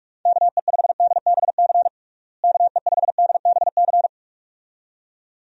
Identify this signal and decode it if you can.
Morse code, heard as a keyed tone, about 46 wpm, 700 Hz. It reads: KE5DBC KE5DBC
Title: USA Call Signs - 6 Characters - Repeated 2x - 45wpm